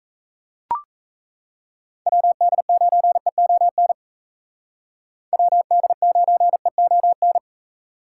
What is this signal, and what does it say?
Morse code, heard as a keyed tone, about 34 wpm, 700 Hz. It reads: WD9EON WD9EON
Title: Speed Racing - USA Call Signs 28wpm